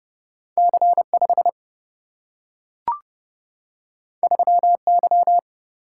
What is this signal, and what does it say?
Morse code, heard as a keyed tone, about 30 wpm, 700 Hz. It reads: C 5 3 Y
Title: Single-Letter Single-Number 30wpm